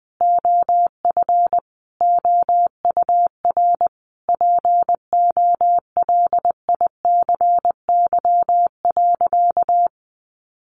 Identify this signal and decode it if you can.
Morse code, heard as a keyed tone, about 19 wpm, 700 Hz. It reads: OF OUR POLICY.
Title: Sentences from Top 1200 Words 20wpm